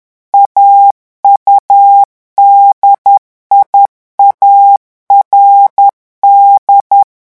Morse code, heard as a keyed tone, about 10 wpm, 800 Hz. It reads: AUDIARD
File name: morse mp3 / revision francais 10 mots minute